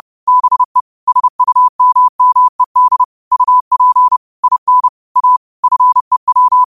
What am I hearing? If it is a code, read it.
Morse code, heard as a keyed tone, about 30 wpm, 1000 Hz. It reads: BE SUMMED UP IN A FEW